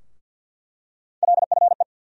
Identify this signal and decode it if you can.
Morse code, heard as a keyed tone, about 47 wpm, 700 Hz. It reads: LFE
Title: USA Call Sign Postfixes - 3 Letters 50wpm